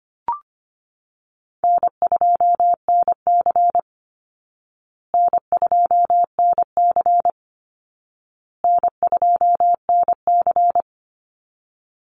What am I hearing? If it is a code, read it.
Morse code, heard as a keyed tone, about 25 wpm, 700 Hz. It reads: N2NC N2NC N2NC
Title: Call Signs Repeated 2x - 25wpm